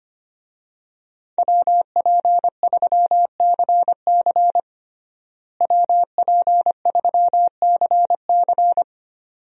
Morse code, heard as a keyed tone, about 25 wpm, 700 Hz. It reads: WP3CC WP3CC
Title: USA Call Signs - 5 Characters - Repeated 2x - 25wpm